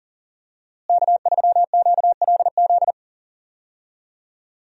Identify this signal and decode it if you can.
Morse code, heard as a keyed tone, about 40 wpm, 700 Hz. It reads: K3QLZ
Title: USA Call Signs - 5 Characters - Repeated 2x - 40wpm